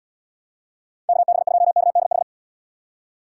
Morse code, heard as a keyed tone, about 74 wpm, 700 Hz. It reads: KB2KDH